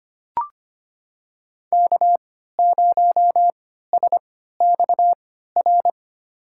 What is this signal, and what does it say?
Morse code, heard as a keyed tone, about 25 wpm, spaced as for 14 wpm, 700 Hz. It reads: K0SXR
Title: USA Call Signs - 5 Characters - 3x Character Spacing - 25wpm